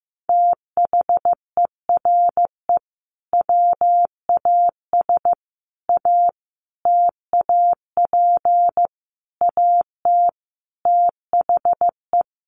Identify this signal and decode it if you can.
Morse code, heard as a keyed tone, about 15 wpm, 700 Hz. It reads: THERE WAS A TAP AT THE
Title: Sentences from Top 1400 Words 15wpm